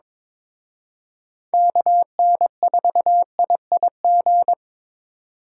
Morse code, heard as a keyed tone, about 26 wpm, 700 Hz. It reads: KN4IIG
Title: Speed Racing - USA 2x3 Call Signs 22wpm